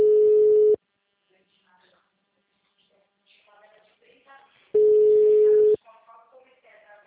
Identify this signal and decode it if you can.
Ringback tone (Europe, 425 Hz), with speech.
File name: out- -2019- - - .121